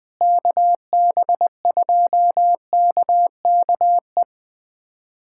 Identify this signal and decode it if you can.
Morse code, heard as a keyed tone, about 20 wpm, 700 Hz. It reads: KB2KKE
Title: USA Callsign DE Callsign - 6 Characters - 20wpm